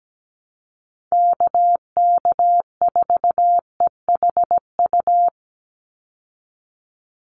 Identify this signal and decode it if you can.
Morse code, heard as a keyed tone, about 17 wpm, 700 Hz. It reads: KK4EHU
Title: USA Call Signs 17wpm